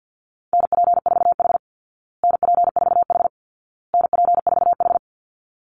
Morse code, heard as a keyed tone, about 50 wpm, 700 Hz. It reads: NP4H NP4H NP4H
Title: Call Signs Repeated 2x - 50wpm